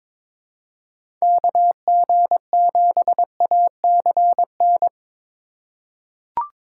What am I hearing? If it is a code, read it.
Morse code, heard as a keyed tone, about 22 wpm, 700 Hz. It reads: KG7ACN
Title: USA Call Signs - 6 Characters - 4x Character Spacing - 22wpm